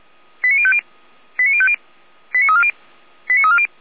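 Five-tone selective call (ZVEI), heard as quick runs of five tones, tones of about 70 ms. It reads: 89050 89050 89340 89340
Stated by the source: ZVEI five-tone selective call received on SDR.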